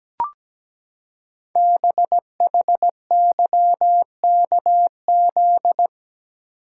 Morse code, heard as a keyed tone, about 17 wpm, 700 Hz. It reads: BHYKZ
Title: Code Groups - 5 Letters - 17wpm